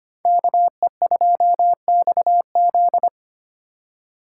Morse code, heard as a keyed tone, about 25 wpm, 700 Hz. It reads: KE2XZ
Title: USA Call Signs - 5 Characters - 2x Character Spacing - 25wpm